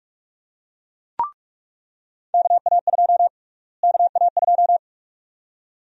Morse code, heard as a keyed tone, about 46 wpm, 700 Hz. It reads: KA2 KA2
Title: USA Call Sign Prefixes - Repeated 2x - 45wpm